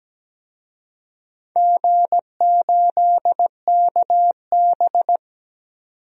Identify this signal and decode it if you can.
Morse code, heard as a keyed tone, about 17 wpm, 700 Hz. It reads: G8KB
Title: DXCC-100 Call Signs - 4 Characters - 17wpm